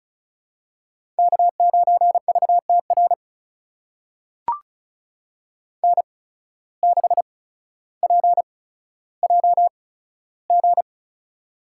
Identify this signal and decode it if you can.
Morse code, heard as a keyed tone, about 36 wpm, spaced as for 15 wpm, 700 Hz. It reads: K9VTR N6PJG
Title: USA Call Signs - 5 Characters - 8x Character Spacing - 35wpm